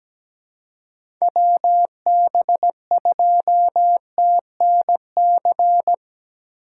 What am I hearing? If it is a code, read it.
Morse code, heard as a keyed tone, about 17 wpm, 700 Hz. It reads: WB2TNC
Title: USA Call Signs - 6 Characters - 4x Character Spacing - 17wpm